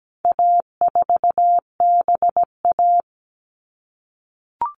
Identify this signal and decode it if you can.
Morse code, heard as a keyed tone, about 17 wpm, 700 Hz. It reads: A4BA
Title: All Cut Numbers - 4-Digit 17wpm